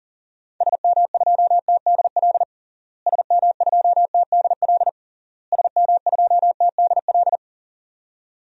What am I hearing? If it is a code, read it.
Morse code, heard as a keyed tone, about 37 wpm, 700 Hz. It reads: SM2TDL SM2TDL SM2TDL
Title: DXCC-100 Call Signs - 6 Characters - Repeated 3x - 40wpm